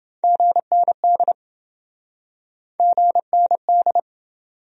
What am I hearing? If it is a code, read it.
Morse code, heard as a keyed tone, about 28 wpm, 700 Hz. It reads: GND GND